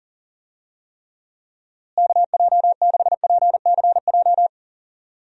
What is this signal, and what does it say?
Morse code, heard as a keyed tone, about 40 wpm, 700 Hz. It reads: KJ6PCJ
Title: USA Call Signs - 6 Characters - Repeated 2x - 40wpm